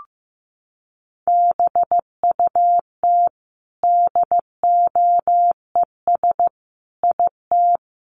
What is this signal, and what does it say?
Morse code, heard as a keyed tone, about 14 wpm, 700 Hz. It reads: BUT DOES IT
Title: Sentences from Top 1200 Words 15wpm